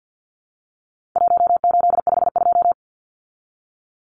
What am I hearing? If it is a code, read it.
Morse code, heard as a keyed tone, about 50 wpm, 700 Hz. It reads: 1852
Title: Numbers - 4-Digit 50wpm